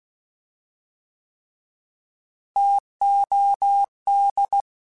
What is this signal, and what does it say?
Morse code, heard as a keyed tone, about 16 wpm, 800 Hz. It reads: TOD